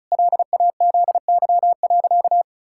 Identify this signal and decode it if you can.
Morse code, heard as a keyed tone, about 36 wpm, 700 Hz. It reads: LAZY.
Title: Pangram Challenge v2 - 35wpm